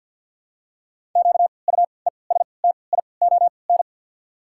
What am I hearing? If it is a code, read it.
Morse code, heard as a keyed tone, about 53 wpm, spaced as for 29 wpm, 690 Hz. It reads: QUESTION
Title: 3x Character Spacing - Top 500 Words - 50wpm